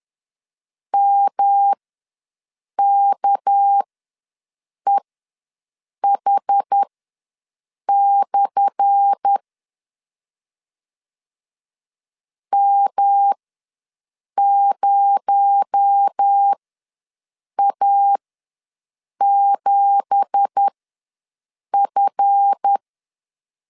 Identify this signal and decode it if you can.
Morse code, heard as a keyed tone, about 10 wpm, spaced as for 6 wpm, 800 Hz. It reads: MKEH/ M0A7F
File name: LEZIONI / 10 SIGNS / 10A SLASH ws 10wpm 4121